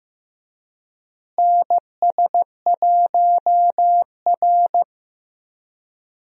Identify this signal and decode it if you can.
Morse code, heard as a keyed tone, about 16 wpm, 700 Hz. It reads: NS1R
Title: USA Call Signs - 4 Characters - Repeated 3x - 15wpm